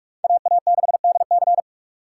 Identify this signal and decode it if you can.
Morse code, heard as a keyed tone, about 44 wpm, 700 Hz. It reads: AA6DC
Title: USA Call Signs - 5 Characters - 4x Character Spacing - 45wpm